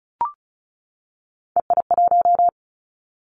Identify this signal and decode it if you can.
Morse code, heard as a keyed tone, about 36 wpm, 700 Hz. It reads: EI1
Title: Callsign Prefixes for Top 50 DXCC - 35wpm